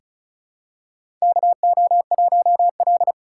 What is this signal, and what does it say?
Morse code, heard as a keyed tone, about 36 wpm, 700 Hz. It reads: KO1L
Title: USA Call Signs - 4 Characters - 4x Character Spacing - 35wpm